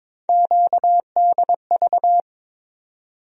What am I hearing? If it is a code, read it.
Morse code, heard as a keyed tone, about 21 wpm, 700 Hz. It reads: QDV
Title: USA Call Sign Postfixes - 3 Letters 22wpm